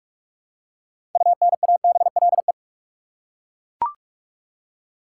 Morse code, heard as a keyed tone, about 42 wpm, 700 Hz. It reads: UNABLE
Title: Six Letter Words - 45wpm